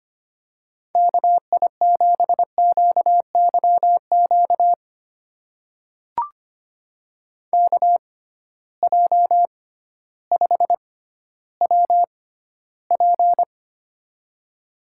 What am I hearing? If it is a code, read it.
Morse code, heard as a keyed tone, about 25 wpm, spaced as for 13 wpm, 700 Hz. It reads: KI7QYQ KJ5WP
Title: USA Call Signs - 6x Character Spacing - 25wpm